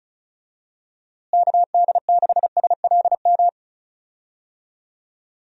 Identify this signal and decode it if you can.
Morse code, heard as a keyed tone, about 36 wpm, 700 Hz. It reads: KD6SLM
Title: USA Call Signs - 6 Characters - Repeated 2x - 35wpm